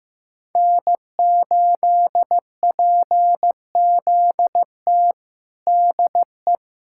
Morse code, heard as a keyed tone, about 15 wpm, 700 Hz. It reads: N8PZT DE
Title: USA Callsign DE Callsign - 5 Characters - 15wpm